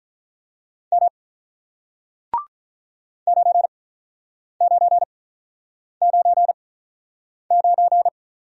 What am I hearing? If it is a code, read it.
Morse code, heard as a keyed tone, about 46 wpm, spaced as for 9 wpm, 690 Hz. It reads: M 9999